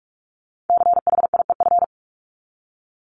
Morse code, heard as a keyed tone, about 44 wpm, 700 Hz. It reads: CHIEF
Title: LY Suffix - Top 100 Words with Base Words 45wpm